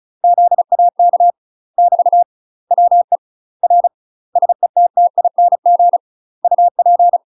Morse code, heard as a keyed tone, about 34 wpm, 700 Hz. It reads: ZAK <BT> WE R SETTING UP
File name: UK MASTER QSOs TRAINING / 35WPM / UK MASTER QSOs 0037